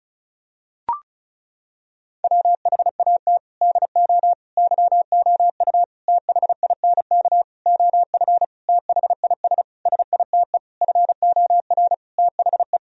Morse code, heard as a keyed tone, about 36 wpm, 690 Hz. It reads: WHAT DO YOU THINK OF THIS SITE FOR THE
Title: Sentences from Top 1000 Words - Review Part 1 - 35wpm